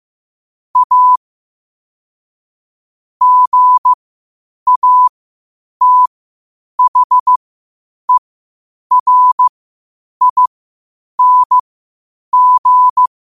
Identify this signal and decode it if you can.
Morse code, heard as a keyed tone, about 15 wpm, spaced as for 8 wpm, 1000 Hz. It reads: A GATHERING
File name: qotdfiles / 8wpm